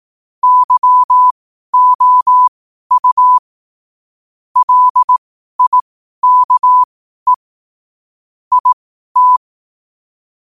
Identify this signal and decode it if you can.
Morse code, heard as a keyed tone, about 18 wpm, spaced as for 12 wpm, 1000 Hz. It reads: YOU LIKE IT
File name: qotdfiles / 12wpm